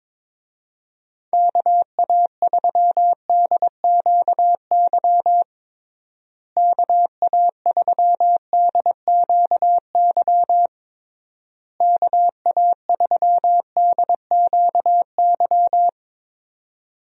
Morse code, heard as a keyed tone, about 22 wpm, 700 Hz. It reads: KA3DQY KA3DQY KA3DQY
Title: USA Call Signs - 6 Characters - Repeated 3x - 22wpm